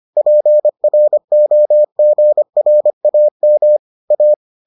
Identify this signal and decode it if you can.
Morse code, heard as a keyed tone, about 25 wpm, 600 Hz. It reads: PROGRAM A